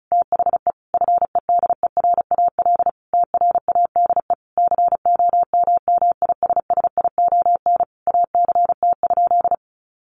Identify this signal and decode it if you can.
Morse code, heard as a keyed tone, about 34 wpm, 700 Hz. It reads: THE FEDERAL TRADE COMMISSION ACT?
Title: Sentences from Top 1400 Words 35wpm